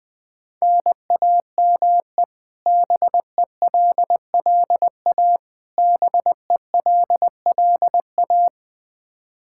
Morse code, heard as a keyed tone, about 21 wpm, 700 Hz. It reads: NAME BELLA BELLA
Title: Practice QSO - Exchange 1 - Beginning - 20wpm